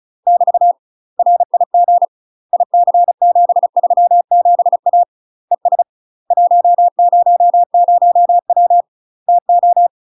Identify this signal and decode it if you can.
Morse code, heard as a keyed tone, about 36 wpm, 700 Hz. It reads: <BT> RIG IC737A ES 100W TO
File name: UK MASTER QSOs TRAINING / 35WPM / UK MASTER QSOs 0018